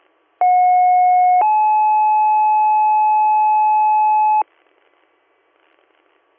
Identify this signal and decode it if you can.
Two-tone page, heard as a one-second tone, then a three-second tone, about 730 then 860 Hz. It is heard from 0.4 to 4.4 s.